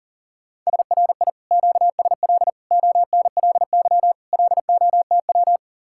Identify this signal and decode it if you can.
Morse code, heard as a keyed tone, about 47 wpm, 700 Hz. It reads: SRI QSL ONLY LOTW
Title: Speed-Racing Common QSO Phrases - 40wpm